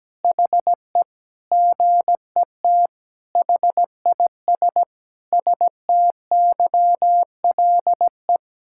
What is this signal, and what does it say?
Morse code, heard as a keyed tone, about 17 wpm, 700 Hz. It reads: HE GET HIS STYLE
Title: Sentences from Top 1000 Words - Review Part 5 - 17wpm